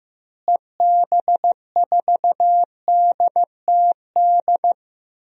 Morse code, heard as a keyed tone, about 15 wpm, 700 Hz. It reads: EB4DTD
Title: DXCC-100 Call Signs - 6 Characters - Repeated 3x - 15wpm